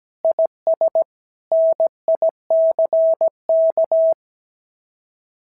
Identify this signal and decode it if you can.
Morse code, heard as a keyed tone, about 17 wpm, 650 Hz. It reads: IS NICK
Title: SOTA QSO Practice 17wpm